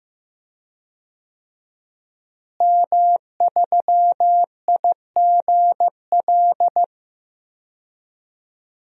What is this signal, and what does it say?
Morse code, heard as a keyed tone, about 15 wpm, 700 Hz. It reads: M3IGL
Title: DXCC-100 Call Signs - 5 Characters - Repeated 2x - 15wpm